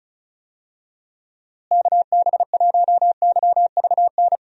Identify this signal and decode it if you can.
Morse code, heard as a keyed tone, about 36 wpm, 700 Hz. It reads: KB1YVN
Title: USA Call Signs - 6 Characters - Repeated 2x - 35wpm